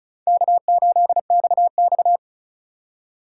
Morse code, heard as a keyed tone, about 36 wpm, 700 Hz. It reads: K8XX
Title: USA Call Signs - 3x Character Spacing - 35wpm